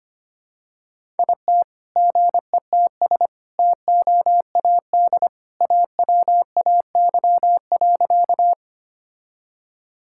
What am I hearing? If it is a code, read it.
Morse code, heard as a keyed tone, about 25 wpm, 700 Hz. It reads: IT GETS TOAD AWAY.
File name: practice / 25wpm / All-Age Appropriate Jokes - 25wpm